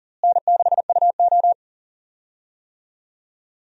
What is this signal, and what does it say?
Morse code, heard as a keyed tone, about 40 wpm, 700 Hz. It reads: N6UO
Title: USA Call Signs - 4 Characters - Repeated 3x - 40wpm